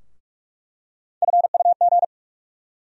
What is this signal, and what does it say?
Morse code, heard as a keyed tone, about 42 wpm, 700 Hz. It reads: FUG
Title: USA Call Sign Postfixes - 3 Letters 45wpm